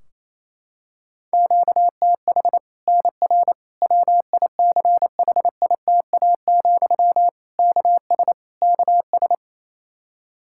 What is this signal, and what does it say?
Morse code, heard as a keyed tone, about 28 wpm, 700 Hz. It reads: QTH NR WICHITA, KS KS
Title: Practice QSO - Exchange 1 - Beginning - 28wpm